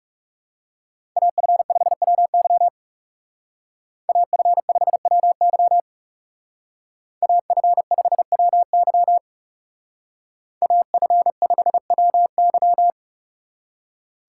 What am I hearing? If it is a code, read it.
Morse code, heard as a keyed tone, about 34 wpm, 700 Hz. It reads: AF5WY AF5WY AF5WY AF5WY
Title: Speed Racing - USA 2x2 Call Signs 30wpm